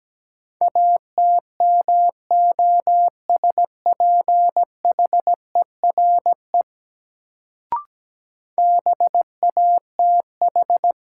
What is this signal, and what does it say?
Morse code, heard as a keyed tone, about 17 wpm, 700 Hz. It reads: ATMOSPHERE BATH
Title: Sentences from Top 1600 Words - New Word Drill Alt - 17wpm